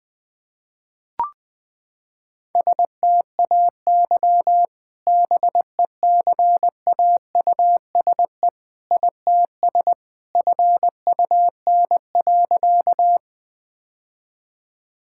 Morse code, heard as a keyed tone, about 20 wpm, 700 Hz. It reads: STAY BECAUSE ITS FUN.
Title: Sentences with 2-Word Phrases - 20wpm